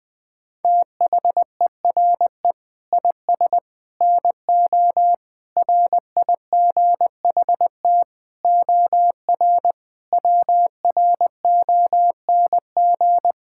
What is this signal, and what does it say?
Morse code, heard as a keyed tone, about 20 wpm, 700 Hz. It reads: THERE IS NO RIGHT OR WRONG
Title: Sentences with 8-Word Phrases - Spoken Beginning - 20wpm